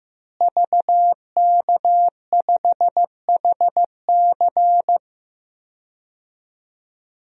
Morse code, heard as a keyed tone, about 15 wpm, 700 Hz. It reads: VK5HC
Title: DXCC-100 Call Signs - 5 Characters - Repeated 3x - 15wpm